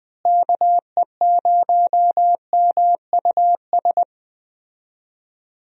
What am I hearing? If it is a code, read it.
Morse code, heard as a keyed tone, about 20 wpm, 700 Hz. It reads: KE0MUS
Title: USA Call Signs - 6 Characters - Repeated 3x - 20wpm